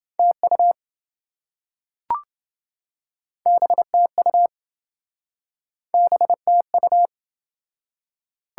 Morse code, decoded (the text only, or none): TU BTU BTU